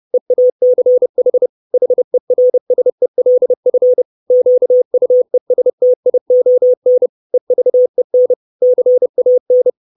Morse code, heard as a keyed tone, about 30 wpm, 500 Hz. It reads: EACH HERSELF QUESTION EVEN CAN
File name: 632 Words 30WPM